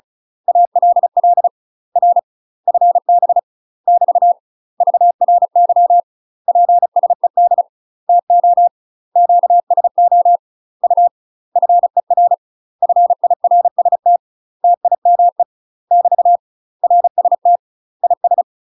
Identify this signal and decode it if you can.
Morse code, heard as a keyed tone, about 36 wpm, 700 Hz. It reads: ALL R FB <BT> VRY PSED TO QSO U FER FIRST TIME <BT> RST IS
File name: UK MASTER QSOs TRAINING / 35WPM / UK MASTER QSOs 0033